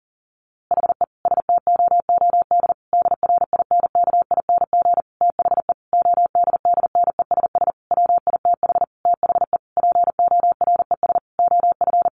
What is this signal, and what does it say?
Morse code, heard as a keyed tone, about 40 wpm, 700 Hz. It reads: HE STOOD DRINKING THE ODDNESS WITH THE PORES OF
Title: A Sound Of Thunder 40wpm